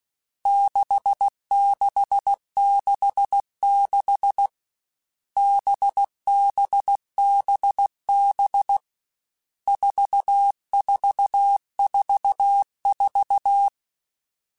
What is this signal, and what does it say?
Morse code, heard as a keyed tone, about 16 wpm, 800 Hz. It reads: 6666 BBBB 4444
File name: Groups@13 / Groups@13 mp3s / VB46@13